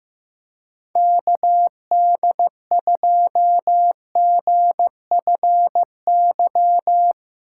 Morse code, heard as a keyed tone, about 15 wpm, 700 Hz. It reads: KD2GFY
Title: USA Call Signs - 6 Characters - 3x Character Spacing - 15wpm